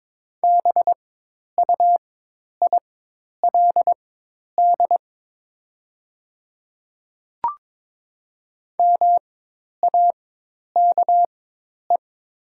Morse code, heard as a keyed tone, about 22 wpm, spaced as for 10 wpm, 700 Hz. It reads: BUILD MAKE
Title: 4x Character Spacing - Top 200 Words - 22wpm